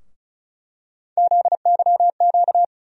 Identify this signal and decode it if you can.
Morse code, heard as a keyed tone, about 36 wpm, 700 Hz. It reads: ZYQ
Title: USA Call Sign Postfixes - 3 Letters 35wpm